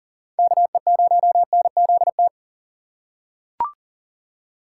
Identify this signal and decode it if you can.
Morse code, heard as a keyed tone, about 40 wpm, 700 Hz. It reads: KE0NZT
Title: USA Call Signs - 6 Characters - 4x Character Spacing - 40wpm